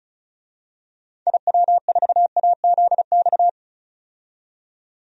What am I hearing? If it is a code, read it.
Morse code, heard as a keyed tone, about 36 wpm, 700 Hz. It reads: IW4AZX
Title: DXCC-100 Call Signs - 6 Characters - 35wpm